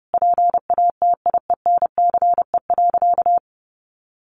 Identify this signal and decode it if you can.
Morse code, heard as a keyed tone, about 30 wpm, 700 Hz. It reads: PATIENCE.
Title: Sentences from Top 1400 Words 30wpm